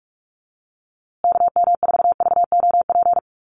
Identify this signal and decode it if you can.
Morse code, heard as a keyed tone, about 44 wpm, 700 Hz. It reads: KM4VOP
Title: USA Call Signs Repeated 2x 45wpm